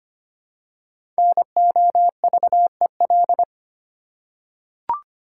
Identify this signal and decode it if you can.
Morse code, heard as a keyed tone, about 25 wpm, 700 Hz. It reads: NOVEL
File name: 5 Letter Words - First 3 Letters Given - 25wpm